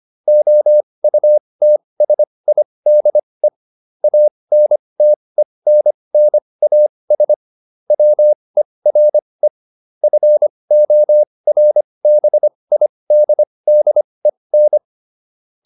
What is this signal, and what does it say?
Morse code, heard as a keyed tone, about 25 wpm, spaced as for 20 wpm, 600 Hz. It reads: OUTSIDE ANTENNAS WERE FORBIDDEN